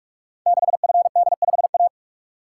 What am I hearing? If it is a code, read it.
Morse code, heard as a keyed tone, about 44 wpm, 690 Hz. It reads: 6FD5A
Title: Code Groups - 5 Letter:Numbers - 45wpm